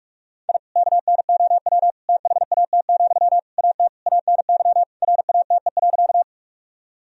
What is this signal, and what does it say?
Morse code, heard as a keyed tone, about 42 wpm, 700 Hz. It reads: I KNOW THAT, AT ANY RATE.
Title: Sentences from Top 1000 Words - Review Part 4 - 45wpm